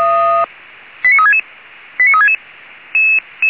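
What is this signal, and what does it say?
Five-tone selective call (ZVEI), heard as quick runs of five tones, tones of about 70 ms. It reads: 89366 89366
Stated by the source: ZVEI five-tone selective call received on an SDR.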